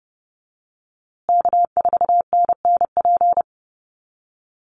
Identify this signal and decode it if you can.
Morse code, heard as a keyed tone, about 30 wpm, 700 Hz. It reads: K4NNP
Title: USA Call Signs Repeated 3x 30wpm